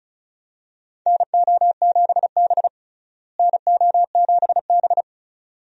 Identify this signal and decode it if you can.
Morse code, heard as a keyed tone, about 36 wpm, 700 Hz. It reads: NO7B NO7B
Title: USA Call Signs - 4 Characters - Repeated 2x - 35wpm